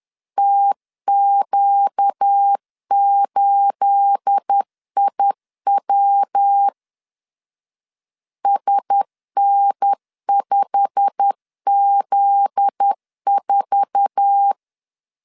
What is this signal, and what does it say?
Morse code, heard as a keyed tone, about 10 wpm, 800 Hz. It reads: TQ8IW SN5Z4
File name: LEZIONI / 9 2867 10wpm 2733